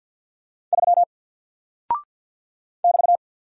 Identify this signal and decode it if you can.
Morse code, heard as a keyed tone, about 48 wpm, 700 Hz. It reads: 3 <BT>
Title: Single Character (Letter:Number:Punctuation:Prosign) Alt 50wpm